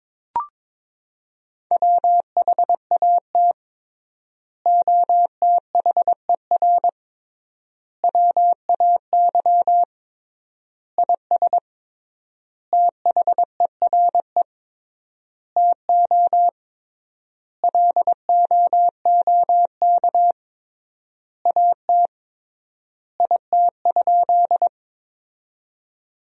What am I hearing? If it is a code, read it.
Morse code, heard as a keyed tone, about 22 wpm, 700 Hz. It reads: WHAT OTHER WAY IS THERE TO LOOK AT IT?
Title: 3x Word-Spacing - Sentences from Top 100 Words - 22wpm